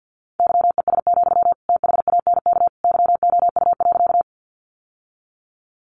Morse code, heard as a keyed tone, about 48 wpm, 700 Hz. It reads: YES, THANK YOU.
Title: Sentences from Top 600 Words 50wpm